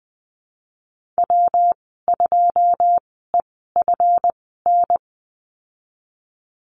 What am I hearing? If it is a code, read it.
Morse code, heard as a keyed tone, about 20 wpm, spaced as for 15 wpm, 700 Hz. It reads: W2EFN
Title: 179 Lesson 29g - Slash - Call Signs 20wpm